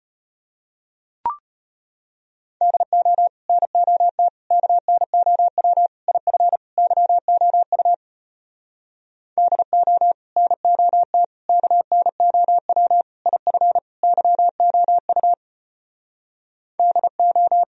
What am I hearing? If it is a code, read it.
Morse code, heard as a keyed tone, about 33 wpm, 700 Hz. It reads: DO NOT KNOW IF YOU DO NOT KNOW IF YOU DO